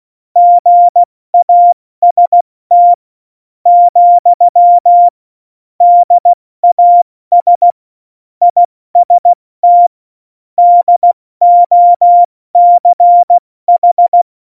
Morse code, heard as a keyed tone, about 16 wpm, spaced as for 13 wpm, 700 Hz. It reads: GAST, DAS IST DOCH
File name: Morsekurs / Witze / Witz 09. April 2019, 70 F80